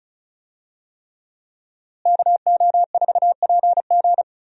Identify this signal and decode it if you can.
Morse code, heard as a keyed tone, about 36 wpm, 700 Hz. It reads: KO4PG
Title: USA Call Signs - 5 Characters - Repeated 3x - 35wpm